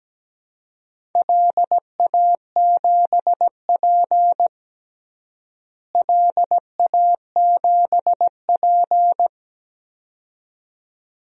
Morse code, heard as a keyed tone, about 17 wpm, 700 Hz. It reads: LA7P LA7P
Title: DXCC-100 Call Signs - 4 Characters - Repeated 2x - 17wpm